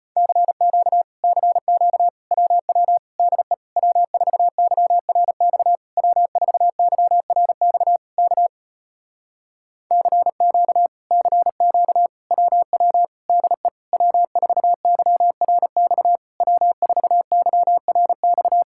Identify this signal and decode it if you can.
Morse code, heard as a keyed tone, about 33 wpm, 700 Hz. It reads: CQ CQ WW DE W4YRX W4YRX K CQ CQ WW DE W4YRX W4YRX